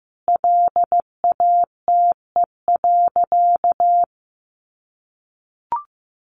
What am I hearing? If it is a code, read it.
Morse code, heard as a keyed tone, about 15 wpm, 700 Hz. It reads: LATE.
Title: Sentences from Top 300 Words 15wpm